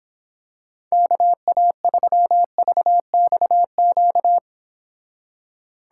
Morse code, heard as a keyed tone, about 26 wpm, 700 Hz. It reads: KA3VXQ